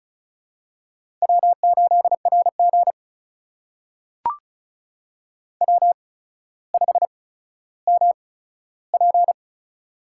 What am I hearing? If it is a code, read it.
Morse code, heard as a keyed tone, about 36 wpm, spaced as for 15 wpm, 700 Hz. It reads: W8RG W5MP
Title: USA Call Signs - 4 Characters - 8x Character Spacing - 35wpm